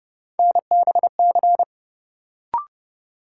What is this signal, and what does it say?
Morse code, heard as a keyed tone, about 30 wpm, 700 Hz. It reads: NBC
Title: Sets of 5 Words Encore - New Word Drill - 30wpm